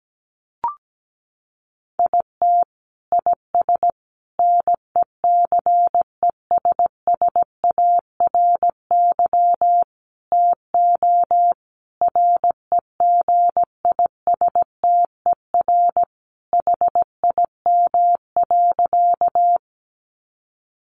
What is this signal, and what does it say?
Morse code, heard as a keyed tone, about 17 wpm, 700 Hz. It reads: IT IS NECESSARY TO REGISTER HIM.
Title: Sentences from Top 1200 Words 17wpm